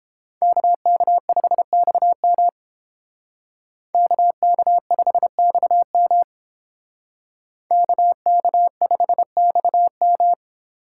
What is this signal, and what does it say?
Morse code, heard as a keyed tone, about 31 wpm, 700 Hz. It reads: KK5XM KK5XM KK5XM